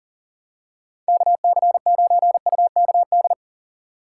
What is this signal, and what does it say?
Morse code, heard as a keyed tone, about 40 wpm, 700 Hz. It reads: KC9UKD
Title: USA Call Signs - 6 Characters - 6x Character Spacing - 40wpm